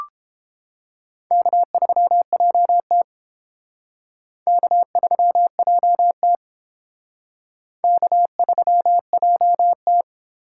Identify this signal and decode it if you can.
Morse code, heard as a keyed tone, about 31 wpm, 700 Hz. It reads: K3JT K3JT K3JT